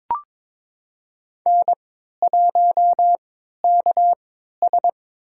Morse code, heard as a keyed tone, about 23 wpm, spaced as for 13 wpm, 700 Hz. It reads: N1KS
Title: USA Call Signs - 4 Characters - 3x Character Spacing - 22wpm